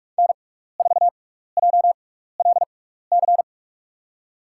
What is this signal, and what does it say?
Morse code, heard as a keyed tone, about 44 wpm, spaced as for 15 wpm, 690 Hz. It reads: N4JLC
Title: USA Call Signs - 5 Characters - 6x Character Spacing - 45wpm